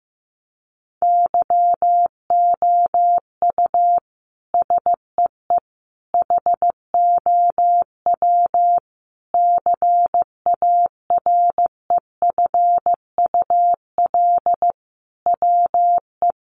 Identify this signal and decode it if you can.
Morse code, heard as a keyed tone, about 15 wpm, 700 Hz. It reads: YOU SEE HOW CAREFUL WE
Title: A Sound Of Thunder 15wpm